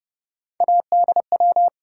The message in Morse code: ADW

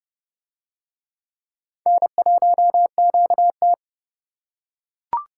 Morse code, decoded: N1QT

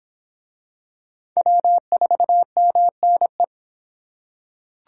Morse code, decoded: W4MNE

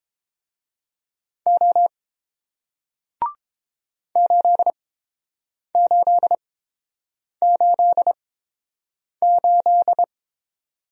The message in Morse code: O 8888

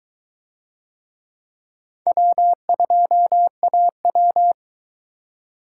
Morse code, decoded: W2AW

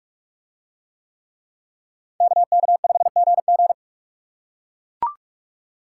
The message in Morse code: KK5GG

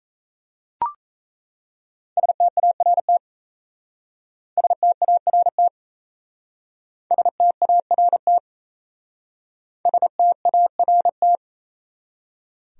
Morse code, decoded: START START START START

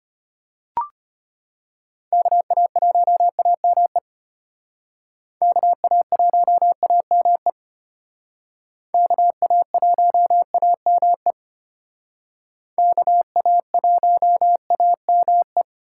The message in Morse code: KA1AME KA1AME KA1AME KA1AME

